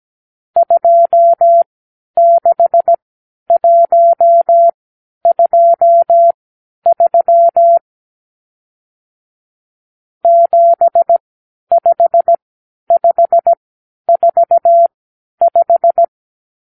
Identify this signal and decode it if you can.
Morse code, heard as a keyed tone, about 17 wpm, spaced as for 10 wpm, 680 Hz. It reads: 26123 75545
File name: k7qo / Lesson 52 (8)